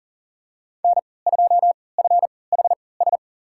Morse code, heard as a keyed tone, about 40 wpm, spaced as for 23 wpm, 700 Hz. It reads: N2FHS
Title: USA Call Signs - 3x Character Spacing - 40wpm